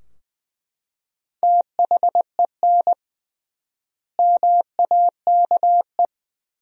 Morse code, heard as a keyed tone, about 21 wpm, 700 Hz. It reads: THEN MAKE
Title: 3x Word-Spacing - Sentences from Top 100 Words - 20wpm